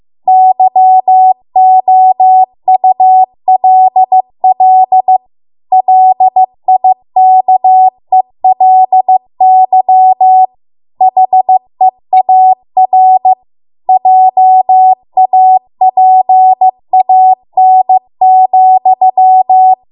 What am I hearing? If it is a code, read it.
Morse code, heard as a keyed tone, about 15 wpm, 750 Hz. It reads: YOULL LIKELY HEAR JAPAN,